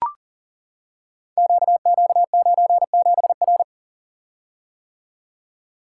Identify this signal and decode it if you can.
Morse code, heard as a keyed tone, about 39 wpm, 690 Hz. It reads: QQ97R
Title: Code Groups - 5 Letter:Numbers - 40wpm